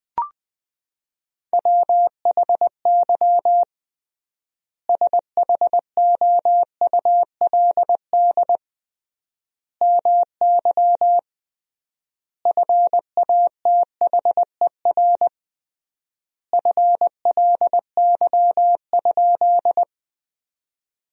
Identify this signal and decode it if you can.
Morse code, heard as a keyed tone, about 20 wpm, 700 Hz. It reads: WHY SHOULD MY FATHER FLY?
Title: 3x Word-Spacing - Sentences from Top 500 Words - 20wpm